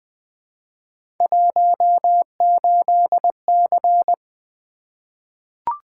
Morse code, heard as a keyed tone, about 20 wpm, 700 Hz. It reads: 18C
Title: Three Letter-Number Combinations - 20wpm